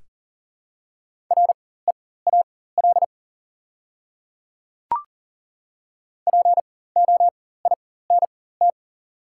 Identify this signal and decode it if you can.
Morse code, heard as a keyed tone, about 40 wpm, spaced as for 19 wpm, 690 Hz. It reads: REAL POINT
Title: 4x Character Spacing - Top 500 Words - 40wpm